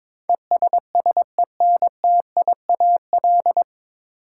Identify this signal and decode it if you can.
Morse code, heard as a keyed tone, about 22 wpm, 700 Hz. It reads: ESSENTIAL
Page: Morse Code Ninja practice 22wpm